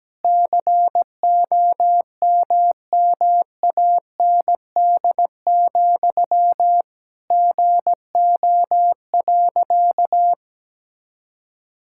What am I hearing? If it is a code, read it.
Morse code, heard as a keyed tone, about 17 wpm, 700 Hz. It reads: COMMAND, GO.
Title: Sentences from Top 1600 Words 17wpm